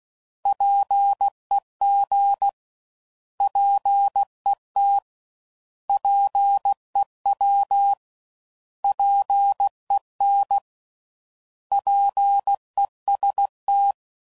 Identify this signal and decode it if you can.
Morse code, heard as a keyed tone, about 16 wpm, 800 Hz. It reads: PEG PET PEW PEN PEST
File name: Prefixes / By First Letter / Ps